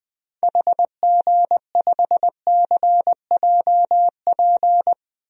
Morse code, heard as a keyed tone, about 20 wpm, 700 Hz. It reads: HG5CJP
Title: DXCC-100 Call Signs - 6 Characters - 20wpm